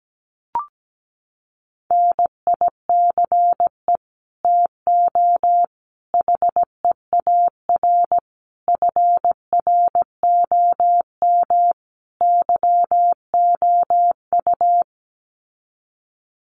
Morse code, heard as a keyed tone, about 17 wpm, 700 Hz. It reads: NICE TO HEAR FROM YOU
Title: Common QSO Phrases 17wpm